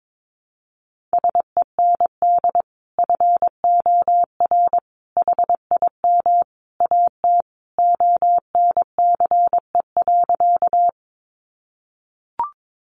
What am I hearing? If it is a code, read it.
Morse code, heard as a keyed tone, about 22 wpm, 700 Hz. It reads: SEND FOR HIM AT ONCE.
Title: Sentences from Top 1500 Words 22wpm